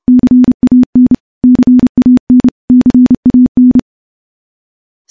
Morse code, heard as a keyed tone, about 28 wpm, 250 Hz. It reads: CAN CAN CAN
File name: morse / morse-training-30WPM